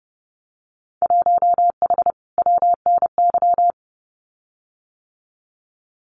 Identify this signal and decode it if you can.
Morse code, heard as a keyed tone, about 29 wpm, 700 Hz. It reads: 1H WNY
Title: Winter Field Day Exchanges Encore 30wpm